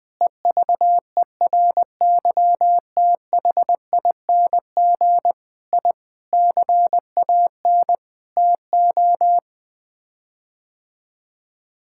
Morse code, heard as a keyed tone, about 21 wpm, 700 Hz. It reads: EVERYTHING I CAN TO
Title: Sets of 8 Words - 20wpm